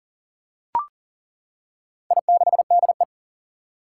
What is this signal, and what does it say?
Morse code, heard as a keyed tone, about 40 wpm, 700 Hz. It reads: I6DE
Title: Four Challenging Letter-Number Combinations 40wpm